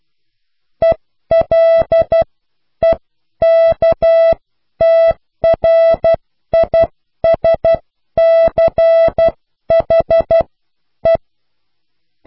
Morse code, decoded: ELEKTRISCHE